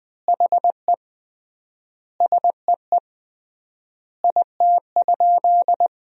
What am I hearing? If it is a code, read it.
Morse code, heard as a keyed tone, about 21 wpm, 700 Hz. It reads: HE SEE IT?
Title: 3x Word-Spacing - Sentences from Top 100 Words - 20wpm